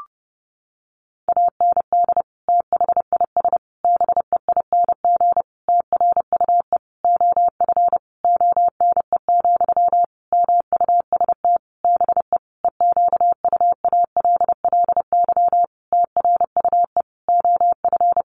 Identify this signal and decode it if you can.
Morse code, heard as a keyed tone, about 28 wpm, 700 Hz. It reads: AND THIS BEING TRUE OF ONE, MUST BE EQUALLY TRUE OF
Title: Sentences from Top 1400 Words 30wpm